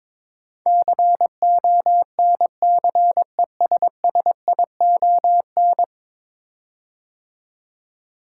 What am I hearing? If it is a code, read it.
Morse code, heard as a keyed tone, about 22 wpm, 700 Hz. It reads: CONCESSION
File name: practice / 22wpm / 10 Letter Words - First 3 Letters Given - 22wpm